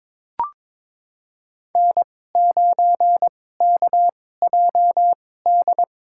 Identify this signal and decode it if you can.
Morse code, heard as a keyed tone, about 21 wpm, spaced as for 16 wpm, 700 Hz. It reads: N9KJD
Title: USA Call Signs - 5 Characters - 2x Character Spacing - 22wpm